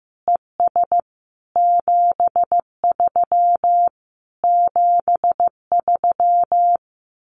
Morse code, heard as a keyed tone, about 15 wpm, 700 Hz. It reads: ES 73 73
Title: Common QSO Phrases Encore 15wpm